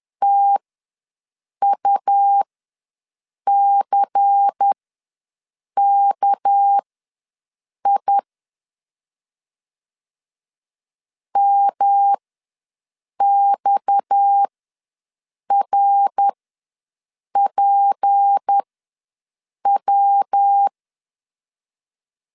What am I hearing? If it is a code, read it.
Morse code, heard as a keyed tone, about 10 wpm, spaced as for 6 wpm, 800 Hz. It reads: TUCKI MXRPW
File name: LEZIONI / 6 PKX0 / 6 PKX ws 10wpm 3706